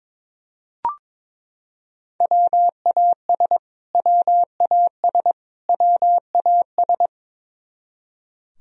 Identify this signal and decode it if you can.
Morse code, heard as a keyed tone, about 22 wpm, 700 Hz. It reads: WAS WAS WAS
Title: Top 100 Words Repeated 22wpm